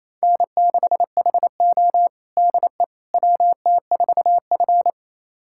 Morse code, heard as a keyed tone, about 28 wpm, 700 Hz. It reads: N6HO DE WT4F
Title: USA Callsign DE Callsign - 4 Characters - 28wpm